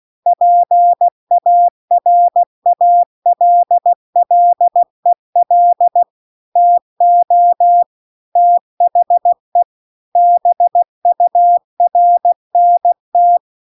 Morse code, heard as a keyed tone, about 16 wpm, 700 Hz. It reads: PARALLEL TO THE BURNT